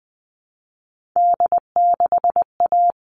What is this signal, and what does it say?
Morse code, heard as a keyed tone, about 20 wpm, 700 Hz. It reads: D6A
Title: All Cut Numbers - 3-Digit 20wpm